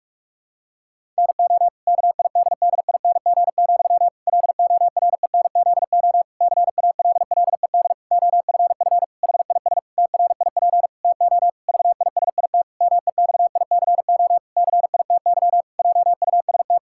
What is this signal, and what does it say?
Morse code, heard as a keyed tone, about 41 wpm, 700 Hz. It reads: NO KIDDING, LORENZO CALLED OFF HIS TRIP TO VISIT MEXICO CITY JUST
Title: Pangram Challenge v2 - 45wpm